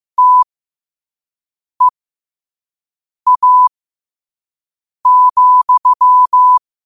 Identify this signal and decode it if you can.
Morse code, heard as a keyed tone, about 15 wpm, spaced as for 5 wpm, 1000 Hz. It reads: TEA,
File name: qotdfiles / 5wpm